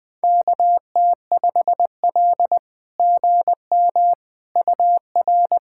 Morse code, heard as a keyed tone, about 20 wpm, 700 Hz. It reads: KT5L GM UR
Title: SOTA QSO Practice 20wpm